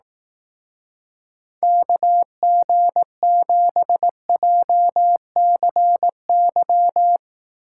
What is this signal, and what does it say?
Morse code, heard as a keyed tone, about 18 wpm, 700 Hz. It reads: KG7JCY
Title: Speed Racing - USA Call Signs 15wpm